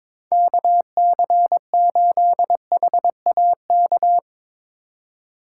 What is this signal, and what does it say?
Morse code, heard as a keyed tone, about 24 wpm, 700 Hz. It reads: KC8HAK
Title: Speed Racing - USA Call Signs 22wpm